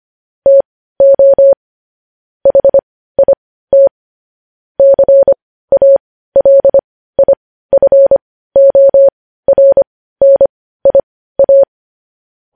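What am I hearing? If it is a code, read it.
Morse code, heard as a keyed tone, about 25 wpm, spaced as for 15 wpm, 550 Hz. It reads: TO HIT CALIFORNIA
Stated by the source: Morse practice sending news headlines at 15 wpm.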